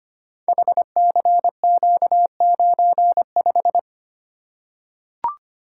Morse code, heard as a keyed tone, about 25 wpm, 700 Hz. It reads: HCQ95
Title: Code Groups - 5 Letter:Numbers - 25wpm